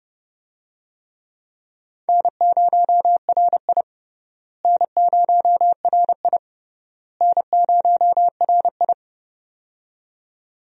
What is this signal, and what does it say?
Morse code, heard as a keyed tone, about 31 wpm, 700 Hz. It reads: N0RI N0RI N0RI
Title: USA Call Signs - 4 Characters - Repeated 3x - 30wpm